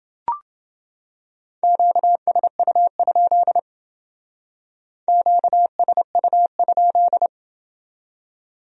Morse code, heard as a keyed tone, about 29 wpm, 700 Hz. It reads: QSU? QSU?